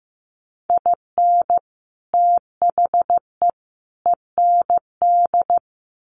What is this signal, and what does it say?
Morse code, heard as a keyed tone, about 15 wpm, 700 Hz. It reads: IN THE END
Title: Sets of 3 Words 15wpm